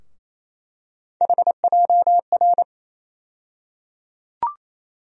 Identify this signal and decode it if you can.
Morse code, heard as a keyed tone, about 28 wpm, 700 Hz. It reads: HJR
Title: USA Call Sign Postfixes - 3 Letters 28wpm